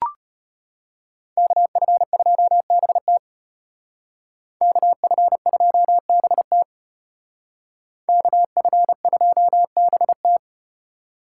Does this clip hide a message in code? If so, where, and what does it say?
Yes: Morse code from 1.4 to 10.4 s, heard as a keyed tone, about 33 wpm, 700 Hz. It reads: KF2BT KF2BT KF2BT